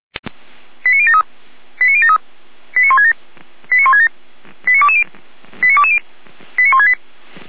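Five-tone selective call (ZVEI), heard as quick runs of five tones, tones of about 70 ms. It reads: 89073 89073 89167 89167 89220 89220 89168
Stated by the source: Five-tone ZVEI selective call 89123 received by SDR, logged by Feuerwehr Orlamünde.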